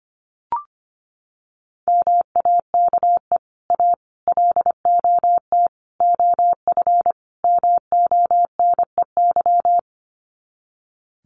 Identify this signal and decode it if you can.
Morse code, heard as a keyed tone, about 25 wpm, 700 Hz. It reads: MAKE A LOT OF MONEY
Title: Sets of 5 Words Encore2 25wpm